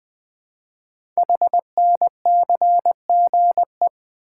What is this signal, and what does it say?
Morse code, heard as a keyed tone, about 20 wpm, 700 Hz. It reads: HNCGE
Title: Code Groups - 5 Letters - 20wpm